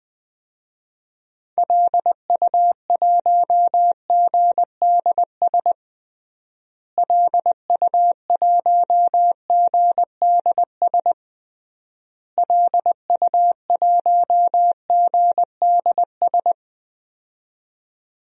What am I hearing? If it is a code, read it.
Morse code, heard as a keyed tone, about 21 wpm, 700 Hz. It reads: LU1GDS LU1GDS LU1GDS
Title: DXCC-100 Call Signs - 6 Characters - Repeated 3x - 20wpm